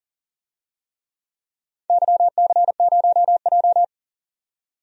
Morse code, heard as a keyed tone, about 40 wpm, 700 Hz. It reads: YC0J
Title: DXCC-100 Call Signs - 4 Characters - 40wpm